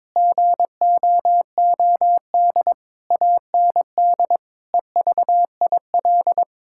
Morse code, heard as a keyed tone, about 22 wpm, 700 Hz. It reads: GOOD AND EVIL
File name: Sets of 9 Words - 22wpm